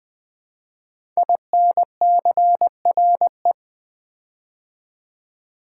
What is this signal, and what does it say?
Morse code, heard as a keyed tone, about 20 wpm, 700 Hz. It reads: INCRE
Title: Build A Word - 5 Letters At A Time - 10-Letter Words - 20wpm